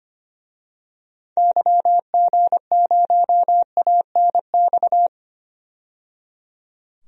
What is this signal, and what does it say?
Morse code, heard as a keyed tone, about 25 wpm, 700 Hz. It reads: YG0ANX
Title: DXCC-100 Call Signs - 6 Characters - 25wpm